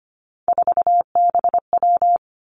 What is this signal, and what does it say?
Morse code, heard as a keyed tone, about 25 wpm, 700 Hz. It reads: 4BW
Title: Three Challenging Letter-Number Combinations 25wpm